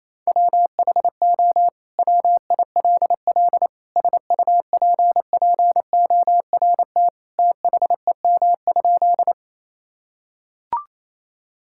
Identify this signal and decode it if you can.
Morse code, heard as a keyed tone, about 27 wpm, 700 Hz. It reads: WHO WILL SUPPORT THEM?
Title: Sentences from Top 1000 Words - Review Part 5 - 28wpm